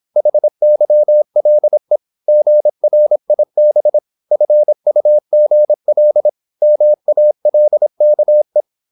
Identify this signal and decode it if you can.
Morse code, heard as a keyed tone, about 26 wpm, 600 Hz. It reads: HYLE GRIB FUGL MALKE
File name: Korte ord DK 26wpm